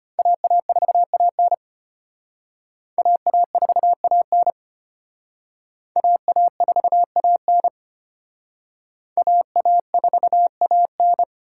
Morse code, heard as a keyed tone, about 32 wpm, 700 Hz. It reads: AA4AN AA4AN AA4AN AA4AN